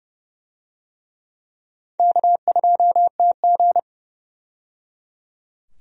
Morse code, heard as a keyed tone, about 30 wpm, 700 Hz. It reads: K2TG
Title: USA Call Signs - 4 Characters - 30wpm